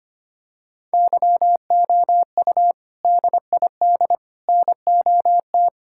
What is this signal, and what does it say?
Morse code, heard as a keyed tone, about 25 wpm, 700 Hz. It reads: YOU DID NOT
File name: practice / 25wpm / Sentences with 3-Word Phrases - 25wpm